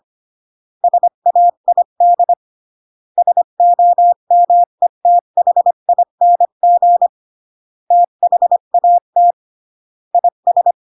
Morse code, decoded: SAID SOMETHING THAT IS